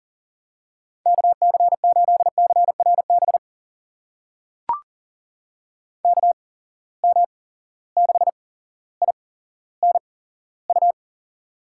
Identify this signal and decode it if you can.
Morse code, heard as a keyed tone, about 38 wpm, spaced as for 17 wpm, 690 Hz. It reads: KC8CRB KM6INU
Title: USA Call Signs - 8x Character Spacing - 40wpm